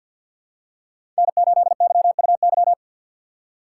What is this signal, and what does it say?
Morse code, heard as a keyed tone, about 53 wpm, 700 Hz. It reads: N8YUY
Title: USA Call Signs - 5 Characters - 3x Character Spacing - 50wpm